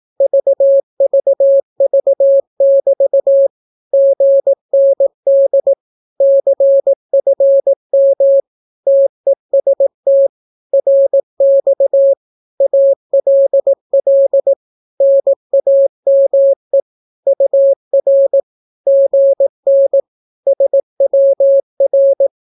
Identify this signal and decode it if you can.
Morse code, heard as a keyed tone, about 18 wpm, 550 Hz. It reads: VVV<BT> GND CFM TEST RX ALL NAME UR GN SWR
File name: naukapodstaw / skroty / skroty